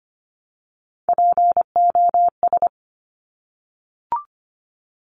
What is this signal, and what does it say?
Morse code, heard as a keyed tone, about 25 wpm, 700 Hz. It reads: POS
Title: Top Three Letter Combinations in Words 25wpm